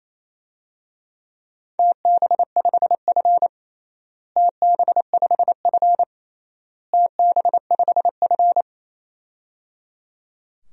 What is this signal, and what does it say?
Morse code, heard as a keyed tone, about 28 wpm, 700 Hz. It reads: TB5F TB5F TB5F
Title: DXCC-100 Call Signs - 4 Characters - Repeated 3x - 28wpm